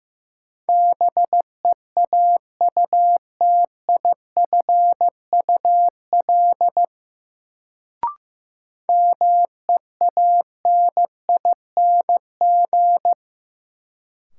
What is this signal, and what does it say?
Morse code, heard as a keyed tone, about 16 wpm, 700 Hz. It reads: BEAUTIFUL MEANING
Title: Sentences from Top 900 Words - New Word Drill - 15wpm